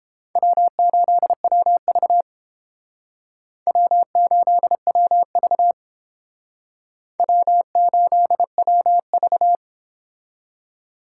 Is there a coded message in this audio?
Yes: Morse code from 0.4 to 9.6 s, heard as a keyed tone, about 31 wpm, 700 Hz. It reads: W8WV W8WV W8WV